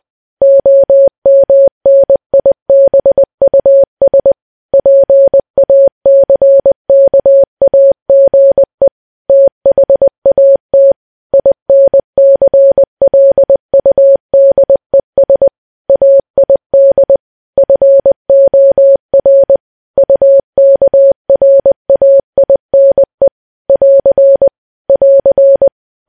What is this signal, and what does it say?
Morse code, heard as a keyed tone, about 20 wpm, 550 Hz. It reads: OMNIBUS PACKAGE THAT INCLUDES AID FOR UKRAINE <AR> <AR>
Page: news-headlines-in-morse-code-at-20-wpm